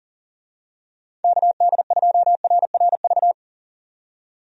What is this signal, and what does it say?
Morse code, heard as a keyed tone, about 40 wpm, 700 Hz. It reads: KD2RRV
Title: USA Call Signs - 6 Characters - 40wpm